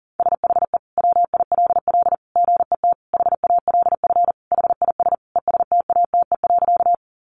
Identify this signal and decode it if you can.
Morse code, heard as a keyed tone, about 40 wpm, 700 Hz. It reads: SHE WILL GET HALF HIS ESTATE.
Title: Sentences from Top 1200 Words 40wpm